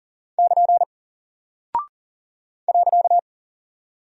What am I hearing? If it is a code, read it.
Morse code, heard as a keyed tone, about 40 wpm, 700 Hz. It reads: <KN> .
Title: Single Character (Letter:Number:Punctuation:Prosign) Alt 40wpm